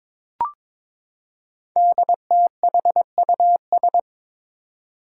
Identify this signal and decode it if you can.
Morse code, heard as a keyed tone, about 22 wpm, 700 Hz. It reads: DTHUS
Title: Five Challenging Letter-Number Combinations 22wpm